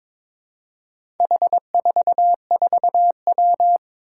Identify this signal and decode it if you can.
Morse code, heard as a keyed tone, about 22 wpm, 700 Hz. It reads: H44W
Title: Four Challenging Letter-Number Combinations 22wpm